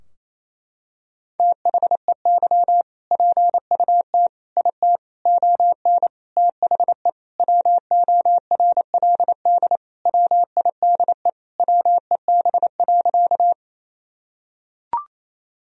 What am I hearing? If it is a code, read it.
Morse code, heard as a keyed tone, about 28 wpm, 700 Hz. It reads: THEY PUT IT ON THE WORLD WIDE WEB.
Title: Sentences from Top 1000 Words - Review Part 4 - 28wpm